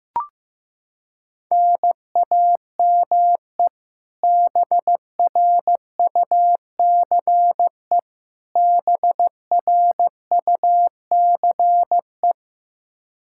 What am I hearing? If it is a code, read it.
Morse code, heard as a keyed tone, about 15 wpm, 700 Hz. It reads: NAME BRUCE BRUCE
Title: Practice QSO - Exchange 1 - Beginning - 15wpm